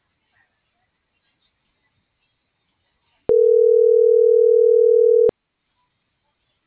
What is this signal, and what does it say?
Ringback tone (North America), 440 + 480 Hz.